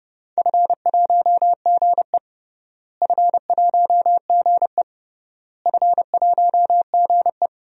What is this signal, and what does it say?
Morse code, heard as a keyed tone, about 32 wpm, 700 Hz. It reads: F1GE F1GE F1GE
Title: DXCC-100 Call Signs - 4 Characters - Repeated 3x - 30wpm